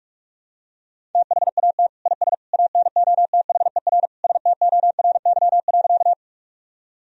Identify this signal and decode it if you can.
Morse code, heard as a keyed tone, about 46 wpm, 700 Hz. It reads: THAT IS ANOTHER STORY.
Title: Sentences from Top 1000 Words - Review Part 1 - 45wpm